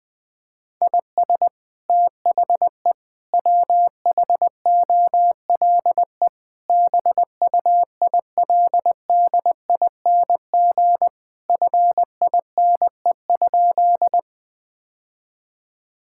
Morse code, heard as a keyed tone, about 20 wpm, 700 Hz. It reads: IS THE WHOLE BUILDING FINE?
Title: Sentences from Top 1000 Words - Review Part 5 - 20wpm